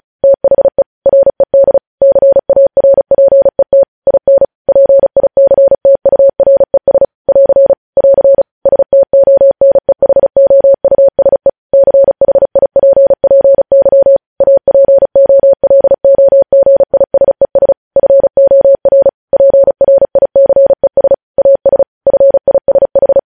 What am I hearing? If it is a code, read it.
Morse code, heard as a keyed tone, about 36 wpm, 550 Hz. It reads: THE RED CARPET IN PICTURES <AR> <AR> STONEHOUSE CHIPPY APOLOGISES FOR PRICES AS FISH
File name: podcast / News headlines-1x-35x35